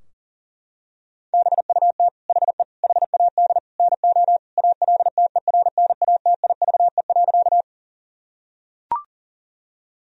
Morse code, heard as a keyed tone, about 40 wpm, 690 Hz. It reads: BUT HE HAD NO ALTERNATIVE.
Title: Sentences from Top 1000 Words - Review Part 4 - 40wpm